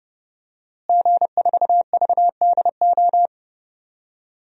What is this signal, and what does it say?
Morse code, heard as a keyed tone, about 30 wpm, 700 Hz. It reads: G4VDO
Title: DXCC-100 Call Signs - 5 Characters - Repeated 3x - 30wpm